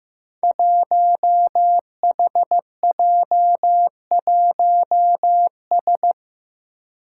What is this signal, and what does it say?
Morse code, heard as a keyed tone, about 15 wpm, 700 Hz. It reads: 1HJ1S
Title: Five Challenging Letter-Number Combinations 15wpm